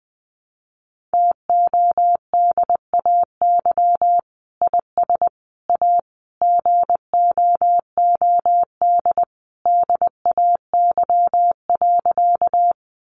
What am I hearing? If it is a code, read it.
Morse code, heard as a keyed tone, about 20 wpm, 700 Hz. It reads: TODAY IS A GOOD DAY.
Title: Sentences from Top 600 Words 20wpm